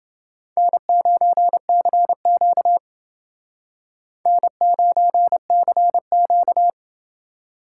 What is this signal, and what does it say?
Morse code, heard as a keyed tone, about 29 wpm, 700 Hz. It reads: N9CQ N9CQ